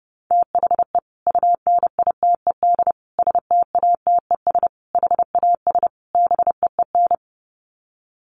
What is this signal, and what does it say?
Morse code, heard as a keyed tone, about 30 wpm, 700 Hz. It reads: THE UNITED STATES HAS BEEN
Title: Sets of 5 Words Encore2 30wpm